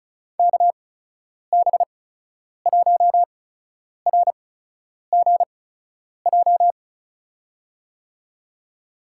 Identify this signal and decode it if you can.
Morse code, heard as a keyed tone, about 36 wpm, spaced as for 10 wpm, 700 Hz. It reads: KB1RGJ
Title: USA Call Signs - 6 Characters - 8x Character Spacing - 35wpm